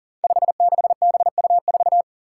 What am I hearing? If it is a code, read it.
Morse code, heard as a keyed tone, about 40 wpm, 700 Hz. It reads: 56BU4
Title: Five Challenging Letter-Number Combinations 40wpm